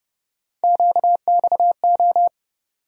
Morse code, heard as a keyed tone, about 30 wpm, 700 Hz. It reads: QXO
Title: USA Call Sign Postfixes - 3 Letters 30wpm